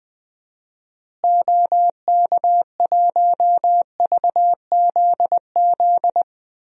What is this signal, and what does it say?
Morse code, heard as a keyed tone, about 20 wpm, 700 Hz. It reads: OK1VZZ
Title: DXCC-100 Call Signs - 6 Characters - Repeated 2x - 20wpm